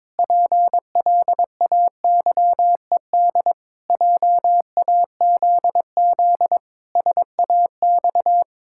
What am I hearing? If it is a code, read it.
Morse code, heard as a keyed tone, about 22 wpm, 700 Hz. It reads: PLAYED JAZZ SAX
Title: Pangram Challenge v2 - 22wpm